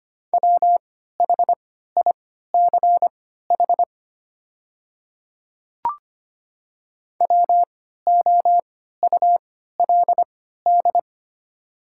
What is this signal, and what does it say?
Morse code, heard as a keyed tone, about 25 wpm, spaced as for 14 wpm, 700 Hz. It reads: WHICH WOULD
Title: 3x Character Spacing - Top 100 Words - 25wpm